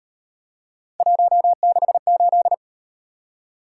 Morse code, heard as a keyed tone, about 36 wpm, 700 Hz. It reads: 168